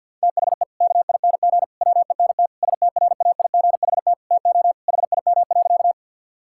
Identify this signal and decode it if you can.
Morse code, heard as a keyed tone, about 50 wpm, 700 Hz. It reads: THE KING WENT STRAIGHT TO HIM.
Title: Sentences from Top 1000 Words - Review Part 6 - 50wpm